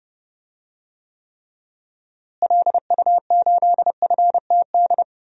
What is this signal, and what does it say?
Morse code, heard as a keyed tone, about 31 wpm, 700 Hz. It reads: LU8FTD
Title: DXCC-100 Call Signs - 6 Characters - Repeated 2x - 30wpm